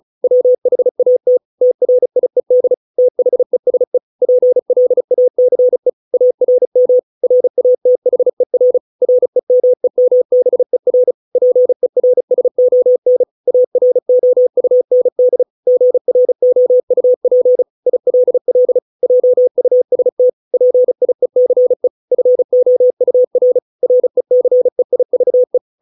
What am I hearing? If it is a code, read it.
Morse code, heard as a keyed tone, about 34 wpm, 500 Hz. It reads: WHAT TRIED THESE PLACE ARM RATHER REMEMBER PERSON AROUND GROUP ILL JUST PIECE FOUR RECEIVE